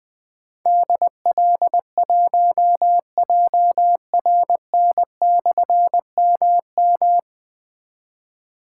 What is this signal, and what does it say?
Morse code, heard as a keyed tone, about 20 wpm, 700 Hz. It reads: DL1JRN/MM
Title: Difficult Call Signs 20wpm